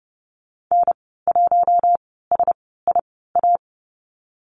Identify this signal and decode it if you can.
Morse code, heard as a keyed tone, about 30 wpm, spaced as for 17 wpm, 700 Hz. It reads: N1SIA
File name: Lesson 15g - Number 5 - Call Signs 30wpm